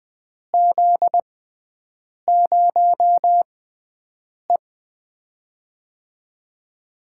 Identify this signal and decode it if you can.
Morse code, heard as a keyed tone, about 21 wpm, spaced as for 7 wpm, 700 Hz. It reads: Z0E
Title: USA Call Signs - 4 Characters - 6x Character Spacing - 20wpm